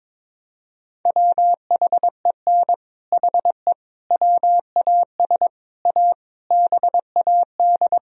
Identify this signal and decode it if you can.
Morse code, heard as a keyed tone, about 22 wpm, 700 Hz. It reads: WHEN HE WAS A BAD
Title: Sentences from Top 1000 Words - Review Part 3 - 22wpm